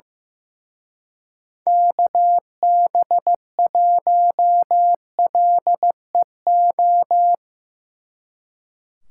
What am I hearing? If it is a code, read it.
Morse code, heard as a keyed tone, about 16 wpm, 700 Hz. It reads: KB1LEO
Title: Speed Racing - USA 2x3 Call Signs 15wpm